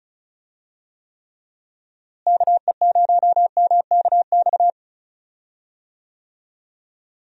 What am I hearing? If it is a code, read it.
Morse code, heard as a keyed tone, about 36 wpm, 700 Hz. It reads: KE0MKX
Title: USA Call Signs - 6 Characters - Repeated 3x - 35wpm